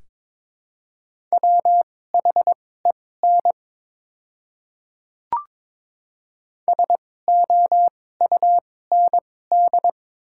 Morse code, heard as a keyed tone, about 22 wpm, spaced as for 16 wpm, 690 Hz. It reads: WHEN SOUND
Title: 2x Character Spacing - Top 100 Words - 22wpm